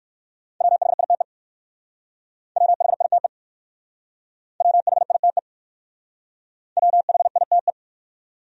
Morse code, heard as a keyed tone, about 54 wpm, 700 Hz. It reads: WHITE WHITE WHITE WHITE
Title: Speed-Racing Top 500 Words 45wpm